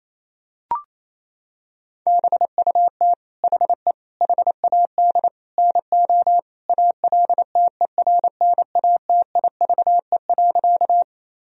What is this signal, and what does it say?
Morse code, heard as a keyed tone, about 28 wpm, 700 Hz. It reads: BUT HE HAD NO ALTERNATIVE.
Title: Sentences from Top 1000 Words - Review Part 4 - 28wpm